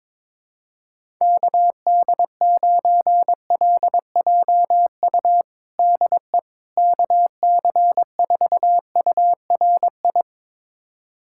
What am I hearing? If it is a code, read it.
Morse code, heard as a keyed tone, about 22 wpm, 700 Hz. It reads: KD9LJU DE KC4URI
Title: USA Callsign DE Callsign 22wpm